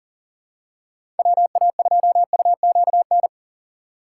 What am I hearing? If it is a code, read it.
Morse code, heard as a keyed tone, about 40 wpm, 700 Hz. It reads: WA2UQN
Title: USA Call Signs - 6 Characters - 6x Character Spacing - 40wpm